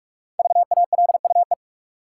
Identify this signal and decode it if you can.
Morse code, heard as a keyed tone, about 45 wpm, 700 Hz. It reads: VALUE
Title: Sentences from Top 600 Words - New Word Drill - 45wpm